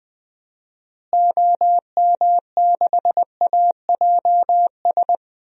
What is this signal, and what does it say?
Morse code, heard as a keyed tone, about 20 wpm, 700 Hz. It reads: OM6AJS
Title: DXCC-100 Call Signs - 6 Characters - Repeated 2x - 20wpm